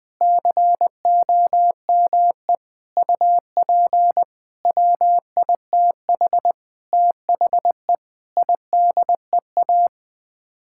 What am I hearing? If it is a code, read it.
Morse code, heard as a keyed tone, about 20 wpm, 700 Hz. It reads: COME UP WITH THE IDEA
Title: Sets of 8 Words - 20wpm